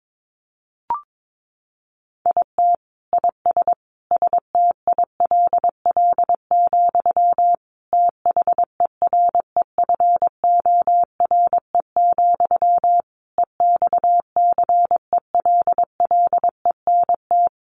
Morse code, heard as a keyed tone, about 22 wpm, 700 Hz. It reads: IT IS STILL, THEREFORE, EXCELLENT
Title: Sentences from Top 1000 Words 22wpm